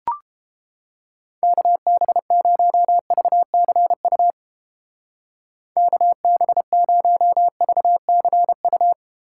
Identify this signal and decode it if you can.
Morse code, heard as a keyed tone, about 29 wpm, 700 Hz. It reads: KB0VCU KB0VCU